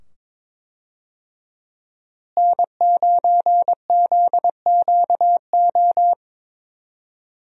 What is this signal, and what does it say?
Morse code, heard as a keyed tone, about 22 wpm, 700 Hz. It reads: N9ZQO
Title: USA Call Signs - 5 Characters - Repeated 2x - 22wpm